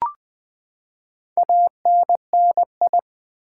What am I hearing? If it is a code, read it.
Morse code, heard as a keyed tone, about 20 wpm, 700 Hz. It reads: ANNI
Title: Build A Word - 4 Letters At A Time - 11-Letter Words - 20wpm